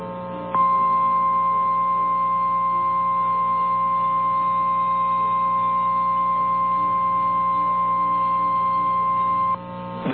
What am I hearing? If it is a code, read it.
Emergency Alert System (SAME) tones (the 1050 Hz attention tone), with noise. It is heard from 0.5 to 9.3 s.
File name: eas / KNCYKBIE / KNCYKBIE